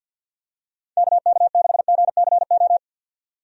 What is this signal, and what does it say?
Morse code, heard as a keyed tone, about 51 wpm, 700 Hz. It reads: KK6GCO
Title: USA Call Signs - 6 Characters - 50wpm